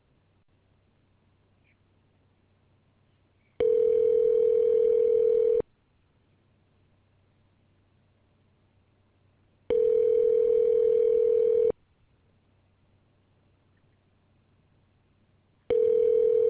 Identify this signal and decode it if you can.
Ringback tone (North America), 440 + 480 Hz.